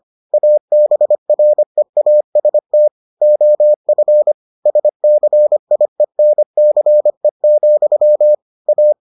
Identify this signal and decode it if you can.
Morse code, heard as a keyed tone, about 25 wpm, 600 Hz. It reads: ABREAST OF SCIENCE, A